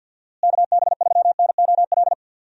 Morse code, heard as a keyed tone, about 51 wpm, 700 Hz. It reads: KB3NOL
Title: USA Call Signs - 6 Characters - Repeated 2x - 50wpm